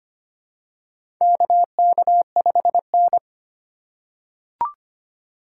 Morse code, heard as a keyed tone, about 25 wpm, 700 Hz. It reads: KK5N
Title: USA Call Signs - 4 Characters - 2x Character Spacing - 25wpm